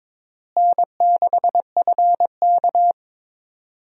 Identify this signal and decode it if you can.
Morse code, heard as a keyed tone, about 22 wpm, 700 Hz. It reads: N6FK
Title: USA Call Signs - 4 Characters - Repeated 3x - 22wpm